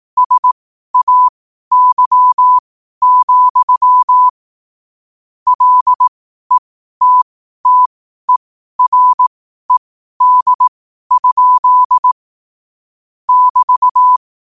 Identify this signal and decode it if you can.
Morse code, heard as a keyed tone, about 18 wpm, spaced as for 12 wpm, 1000 Hz. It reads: SAY, LETTERED? <BT>
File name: qotdfiles / 12wpm